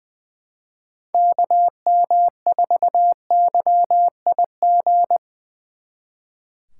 Morse code, heard as a keyed tone, about 23 wpm, 700 Hz. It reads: KM4YIG
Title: Speed Racing - USA 2x3 Call Signs 20wpm